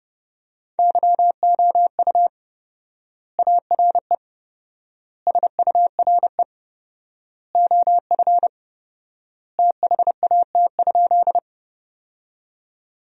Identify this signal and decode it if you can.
Morse code, heard as a keyed tone, about 30 wpm, 700 Hz. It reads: YOU ARE SURE OF THAT?
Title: 4x Word-Spacing - Sentences from Top 300 Words - 30wpm